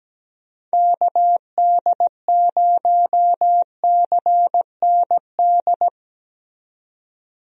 Morse code, heard as a keyed tone, about 17 wpm, 700 Hz. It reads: KD0CND
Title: USA Call Signs - 6 Characters - 17wpm